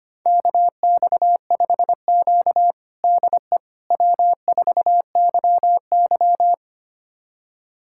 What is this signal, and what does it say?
Morse code, heard as a keyed tone, about 25 wpm, 700 Hz. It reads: KX5Q DE W4YY
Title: USA Callsign DE Callsign - 4 Characters - 25wpm